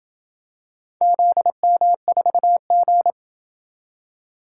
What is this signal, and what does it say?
Morse code, heard as a keyed tone, about 27 wpm, 700 Hz. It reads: ZM4G